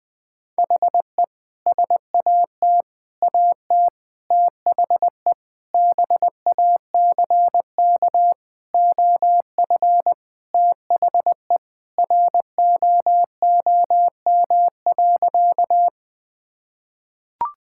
Morse code, heard as a keyed tone, about 20 wpm, 700 Hz. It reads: HE SAT AT THE BACK OF THE ROOM.
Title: Sentences with 3-Word Phrases - 20wpm